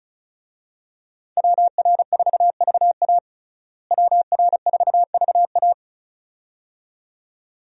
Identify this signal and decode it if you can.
Morse code, heard as a keyed tone, about 36 wpm, 700 Hz. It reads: WR4VA WR4VA
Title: USA Call Signs - 5 Characters - Repeated 2x - 35wpm